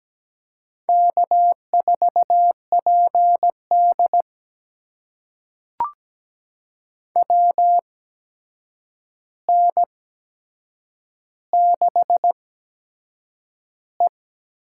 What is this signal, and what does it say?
Morse code, heard as a keyed tone, about 17 wpm, spaced as for 7 wpm, 700 Hz. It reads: K4PD WN6E
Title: USA Call Signs - 4 Characters - 8x Character Spacing - 17wpm